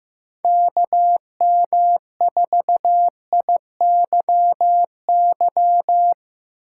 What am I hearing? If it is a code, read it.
Morse code, heard as a keyed tone, about 16 wpm, 700 Hz. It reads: KM4IYY
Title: Speed Racing - USA Call Signs 15wpm